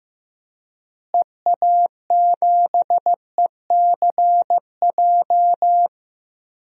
Morse code, heard as a keyed tone, about 15 wpm, 700 Hz. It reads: EA7ECJ
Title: DXCC-100 Call Signs - 6 Characters - Repeated 3x - 15wpm